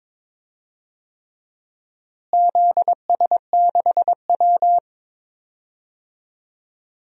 Morse code, heard as a keyed tone, about 22 wpm, 700 Hz. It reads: ZS6W
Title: DXCC-100 Call Signs - 4 Characters - Repeated 3x - 22wpm